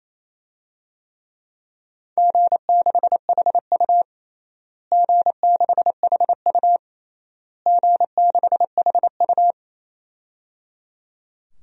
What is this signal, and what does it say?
Morse code, heard as a keyed tone, about 27 wpm, 700 Hz. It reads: G6HU G6HU G6HU
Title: DXCC-100 Call Signs - 4 Characters - Repeated 3x - 28wpm